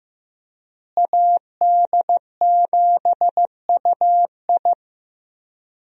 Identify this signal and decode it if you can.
Morse code, heard as a keyed tone, about 15 wpm, 700 Hz. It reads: AD7UI
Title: USA Call Signs - 5 Characters - Repeated 3x - 15wpm